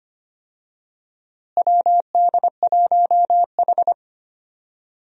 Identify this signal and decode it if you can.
Morse code, heard as a keyed tone, about 25 wpm, 700 Hz. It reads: WD1H
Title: USA Call Signs - 4 Characters - Repeated 2x - 25wpm